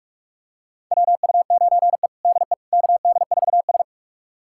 Morse code, heard as a keyed tone, about 41 wpm, 700 Hz. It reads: WU9E DE KD4S
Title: USA Callsign DE Callsign - 4 Characters - 45wpm